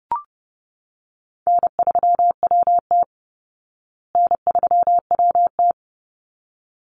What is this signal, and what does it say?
Morse code, heard as a keyed tone, about 30 wpm, 700 Hz. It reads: N3WT N3WT
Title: Call Signs Repeated 3x - 30wpm